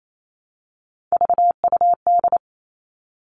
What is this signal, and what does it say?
Morse code, heard as a keyed tone, about 28 wpm, 700 Hz. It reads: VUD
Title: Three Challenging Letter-Number Combinations 28wpm